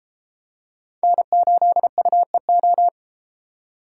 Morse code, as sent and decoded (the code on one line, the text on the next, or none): -. ---.. ..- . ---
N8UEO